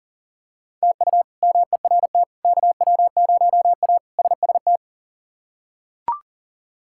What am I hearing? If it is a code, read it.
Morse code, heard as a keyed tone, about 39 wpm, 700 Hz. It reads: TU MERT KW0A SST
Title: K1USN SST QSO Practice - 40wpm